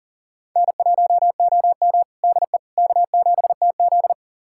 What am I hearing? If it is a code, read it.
Morse code, heard as a keyed tone, about 40 wpm, 700 Hz. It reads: N1OM DE K7TZ
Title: USA Callsign DE Callsign - 4 Characters - 40wpm